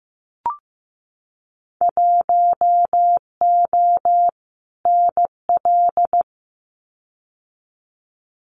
Morse code, heard as a keyed tone, about 15 wpm, 700 Hz. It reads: 1O NL
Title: Winter Field Day Exchanges Encore 15wpm